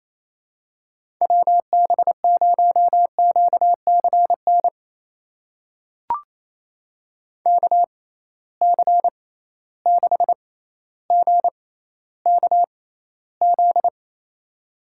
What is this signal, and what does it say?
Morse code, heard as a keyed tone, about 28 wpm, spaced as for 14 wpm, 700 Hz. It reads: WB0QCN KC6GKZ
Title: USA Call Signs - 6 Characters - 6x Character Spacing - 28wpm